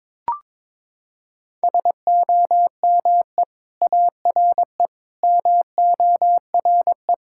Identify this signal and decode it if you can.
Morse code, heard as a keyed tone, about 22 wpm, 700 Hz. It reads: SOME ARE MORE
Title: Sentences from Top 1000 Words - Review Part 6 - 22wpm